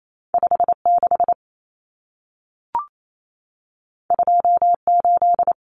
Morse code, heard as a keyed tone, about 28 wpm, 700 Hz. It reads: 56 28
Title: Numbers - 2-Digit 28wpm